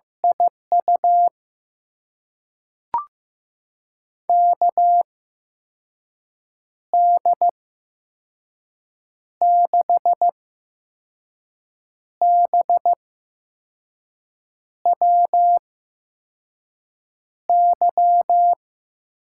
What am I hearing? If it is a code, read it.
Morse code, heard as a keyed tone, about 16 wpm, spaced as for 5 wpm, 700 Hz. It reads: IU KD6BWY
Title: USA Call Signs - 6 Characters - 8x Character Spacing - 15wpm